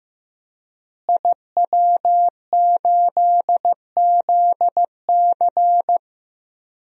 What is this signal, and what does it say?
Morse code, heard as a keyed tone, about 15 wpm, 700 Hz. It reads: IW8ZC
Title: DXCC-100 Call Signs - 5 Characters - Repeated 3x - 15wpm